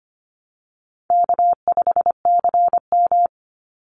Morse code, heard as a keyed tone, about 25 wpm, 700 Hz. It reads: K5CM
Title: Easy Call Signs Repeated 3x - 25wpm